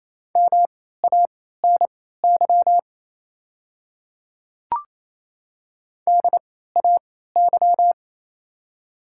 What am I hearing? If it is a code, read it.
Morse code, heard as a keyed tone, about 28 wpm, spaced as for 16 wpm, 700 Hz. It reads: MANY DAY
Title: 3x Character Spacing - Top 100 Words - 28wpm